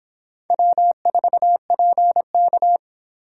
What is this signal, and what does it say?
Morse code, heard as a keyed tone, about 25 wpm, 700 Hz. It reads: W4PK